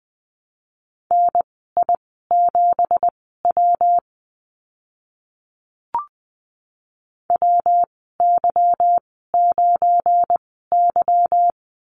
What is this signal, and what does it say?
Morse code, heard as a keyed tone, about 20 wpm, spaced as for 14 wpm, 700 Hz. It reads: NI7W WY9Y
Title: 238 Lesson 38g - Letter q - Call Signs 20wpm